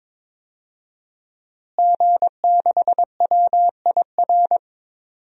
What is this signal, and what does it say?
Morse code, heard as a keyed tone, about 22 wpm, 700 Hz. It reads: G6WIR
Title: DXCC-100 Call Signs - 5 Characters - Repeated 3x - 22wpm